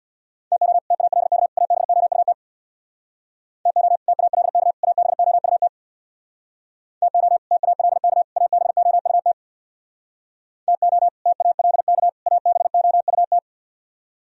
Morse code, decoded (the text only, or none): TO TALK ABOUT TO TALK ABOUT TO TALK ABOUT TO TALK ABOUT